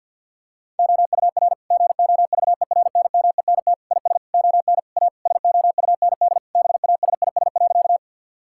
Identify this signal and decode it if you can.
Morse code, heard as a keyed tone, about 50 wpm, 700 Hz. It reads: OUR GOVERNMENT IS ON A SOUND BASIS.
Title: Sentences from Top 1000 Words - Review Part 5 - 50wpm